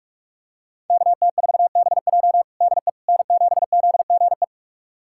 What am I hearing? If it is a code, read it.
Morse code, heard as a keyed tone, about 46 wpm, 700 Hz. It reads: KT4BJ DE N7ZGE
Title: USA Callsign DE Callsign - 5 Characters - 45wpm